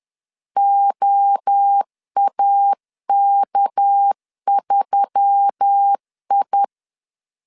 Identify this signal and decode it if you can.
Morse code, heard as a keyed tone, about 10 wpm, 800 Hz. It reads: OAK3I
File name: CW / DET / 1 6 1500C S B 10wpm 4137